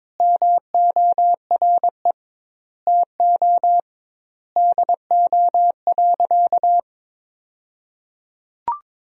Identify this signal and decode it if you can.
Morse code, heard as a keyed tone, about 21 wpm, 700 Hz. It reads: MORE TO DO.
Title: 2x Word-Spacing - Sentences from Top 100 Words - 22wpm